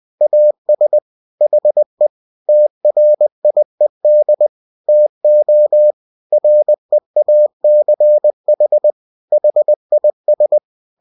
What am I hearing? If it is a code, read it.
Morse code, heard as a keyed tone, about 20 wpm, 600 Hz. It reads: AS HE TRIED TO REACH HIS